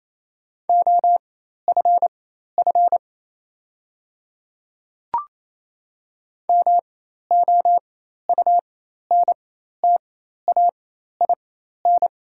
Morse code, heard as a keyed tone, about 28 wpm, spaced as for 13 wpm, 700 Hz. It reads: OFF MOUNTAIN
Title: 4x Character Spacing - Top 300 Words - 28wpm